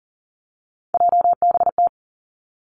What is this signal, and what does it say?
Morse code, heard as a keyed tone, about 42 wpm, 700 Hz. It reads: JBT
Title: Three Challenging Letter-Number Combinations 40wpm